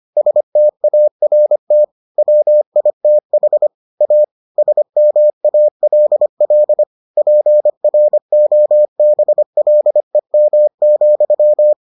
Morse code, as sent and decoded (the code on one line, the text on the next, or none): ... - .- .-. - / .-- .. - .... / .- / ... -- .- .-.. .-.. / .--. .-. --- -... .-.. -- --..--
START WITH A SMALL PROBLEM,